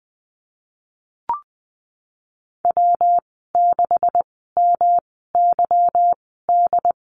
Morse code, heard as a keyed tone, about 20 wpm, spaced as for 14 wpm, 700 Hz. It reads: W6MYD
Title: 179 Lesson 29g - Slash - Call Signs 20wpm